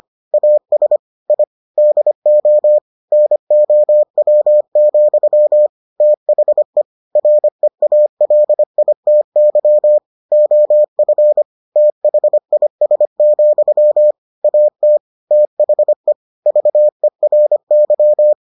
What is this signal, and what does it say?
Morse code, heard as a keyed tone, about 25 wpm, 600 Hz. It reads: AS I DO NOW, THE REALITY OF THIS, AT THE VERY